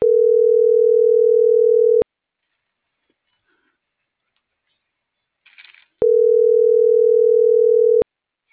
Ringback tone (North America), 440 + 480 Hz.
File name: CENTRAL NMP / out erasto